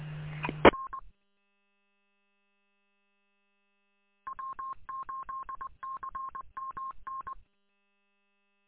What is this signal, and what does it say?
Morse code, heard as a keyed tone, about 20 wpm, 1070 Hz. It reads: R W8CMN
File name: w8cmn- -combined